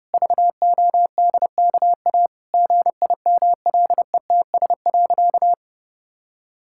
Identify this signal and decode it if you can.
Morse code, heard as a keyed tone, about 29 wpm, 700 Hz. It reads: VODKA GIMLETS.
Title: Pangram Challenge v2 - 30wpm